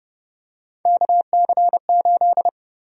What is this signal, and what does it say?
Morse code, heard as a keyed tone, about 30 wpm, 700 Hz. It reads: KC8
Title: USA Call Sign Prefixes - 30wpm